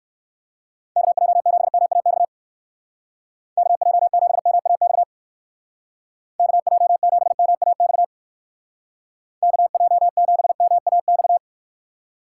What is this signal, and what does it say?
Morse code, heard as a keyed tone, about 56 wpm, 700 Hz. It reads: KJ7MAX KJ7MAX KJ7MAX KJ7MAX